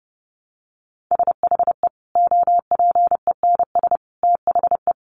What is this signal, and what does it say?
Morse code, heard as a keyed tone, about 30 wpm, 700 Hz. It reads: SHE OPENS THE
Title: Additional Easy Phrases 30wpm